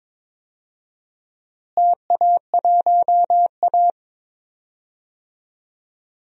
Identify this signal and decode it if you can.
Morse code, heard as a keyed tone, about 21 wpm, 700 Hz. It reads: TA1A
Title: DXCC-100 Call Signs - 4 Characters - Repeated 3x - 22wpm